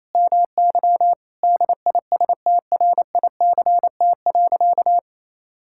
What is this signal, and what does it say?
Morse code, heard as a keyed tone, about 28 wpm, 700 Hz. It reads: MY DISTRICT.
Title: Sentences from Top 1600 Words 28wpm